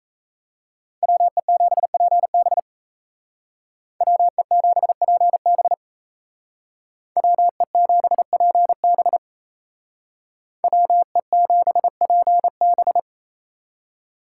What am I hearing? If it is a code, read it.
Morse code, heard as a keyed tone, about 34 wpm, 700 Hz. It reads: WE7PB WE7PB WE7PB WE7PB
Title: Speed Racing - USA 2x2 Call Signs 28wpm